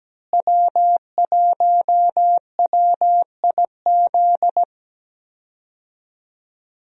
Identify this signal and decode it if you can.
Morse code, heard as a keyed tone, about 17 wpm, 700 Hz. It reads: W1WIZ
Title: USA Call Signs - 5 Characters - Repeated 3x - 17wpm